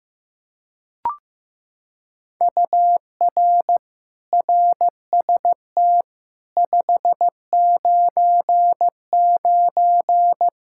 Morse code, heard as a keyed tone, about 15 wpm, 700 Hz. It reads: UR RST 599
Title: Practice QSO - Exchange 1 - Beginning - 15wpm